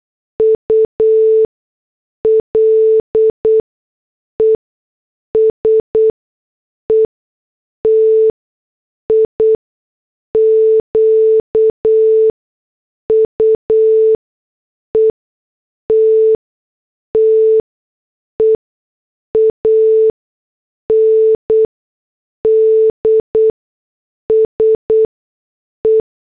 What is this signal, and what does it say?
Morse code, heard as a keyed tone, about 8 wpm, spaced as for 6 wpm, 440 Hz. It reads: ULES ETIQUETTE AND SE